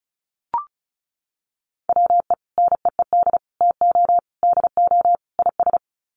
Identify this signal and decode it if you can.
Morse code, heard as a keyed tone, about 34 wpm, 700 Hz. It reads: WE NEED TO DO IS
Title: Sets of 5 Words Encore2 35wpm